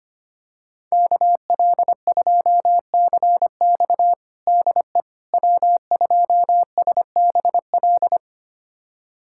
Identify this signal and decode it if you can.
Morse code, heard as a keyed tone, about 25 wpm, 700 Hz. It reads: KL2CX DE W2SBL
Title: USA Callsign DE Callsign - 5 Characters - 25wpm